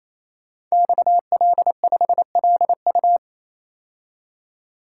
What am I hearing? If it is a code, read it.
Morse code, heard as a keyed tone, about 29 wpm, 700 Hz. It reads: XL5LU
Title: Code Groups - 5 Letter:Numbers - 28wpm